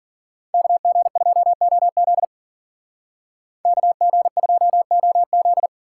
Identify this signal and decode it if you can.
Morse code, heard as a keyed tone, about 48 wpm, 700 Hz. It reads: KG2OZ KG2OZ
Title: Speed Racing - USA 2x2 Call Signs 40wpm